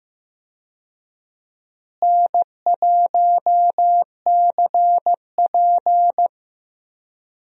Morse code, heard as a keyed tone, about 15 wpm, 700 Hz. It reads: N1CP
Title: USA Call Signs - 4 Characters - Repeated 2x - 15wpm